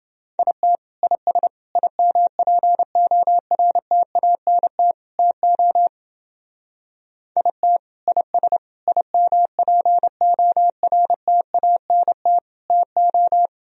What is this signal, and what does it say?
Morse code, heard as a keyed tone, about 28 wpm, 700 Hz. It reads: IT IS IMPORTANT TO IT IS IMPORTANT TO